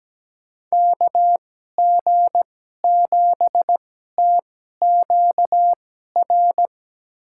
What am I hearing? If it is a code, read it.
Morse code, heard as a keyed tone, about 17 wpm, spaced as for 12 wpm, 700 Hz. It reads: KG7TQR
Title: USA Call Signs - 6 Characters - 2x Character Spacing - 17wpm